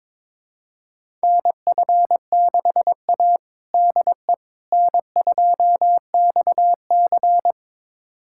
Morse code, heard as a keyed tone, about 22 wpm, 700 Hz. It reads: NF6A DE N2XC
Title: USA Callsign DE Callsign - 4 Characters - 22wpm